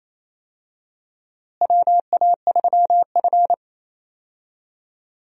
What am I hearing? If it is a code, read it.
Morse code, heard as a keyed tone, about 28 wpm, 700 Hz. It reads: WA3F
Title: USA Call Signs - 4 Characters - Repeated 3x - 28wpm